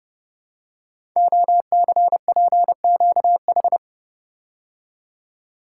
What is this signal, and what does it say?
Morse code, heard as a keyed tone, about 30 wpm, 700 Hz. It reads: OCPQH
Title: Code Groups - 5 Letter:Numbers - 30wpm